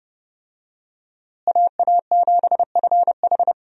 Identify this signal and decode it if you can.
Morse code, heard as a keyed tone, about 32 wpm, 700 Hz. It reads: AA7FH
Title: USA Call Signs - 5 Characters - Repeated 3x - 30wpm